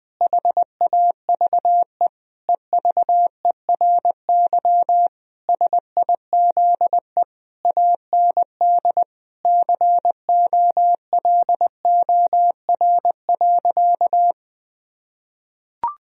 Morse code, heard as a keyed tone, about 20 wpm, 700 Hz. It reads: HAVE EVERY SIZE AND COLOR.
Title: Sentences from Top 1000 Words - Review Part 2 - 20wpm